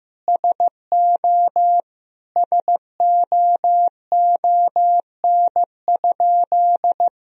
Morse code, heard as a keyed tone, about 15 wpm, 700 Hz. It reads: SO SOON?
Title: Sentences from Top 400 Words and Abbreviations 15wpm